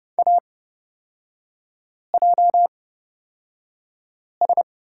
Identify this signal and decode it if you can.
Morse code, heard as a keyed tone, about 29 wpm, spaced as for 5 wpm, 690 Hz. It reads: AJS
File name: Lesson 38e - Letter q - Three Characters 30wpm